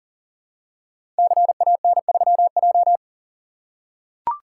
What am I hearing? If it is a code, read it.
Morse code, heard as a keyed tone, about 42 wpm, 700 Hz. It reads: CAN3J
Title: Code Groups - 5 Letter:Numbers - 40wpm